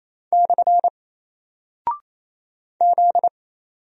Morse code, heard as a keyed tone, about 28 wpm, 700 Hz. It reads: / Z
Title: Single Character (Letter:Number:Punctuation:Prosign) Alt 28wpm